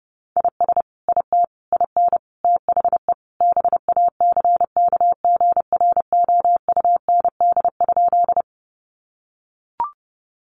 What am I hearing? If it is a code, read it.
Morse code, heard as a keyed tone, about 30 wpm, 700 Hz. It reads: IS IT IN THE BACKGROUND?
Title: Sentences from Top 1300 Words 30wpm